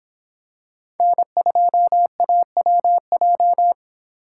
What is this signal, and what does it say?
Morse code, heard as a keyed tone, about 26 wpm, 700 Hz. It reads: N2AWJ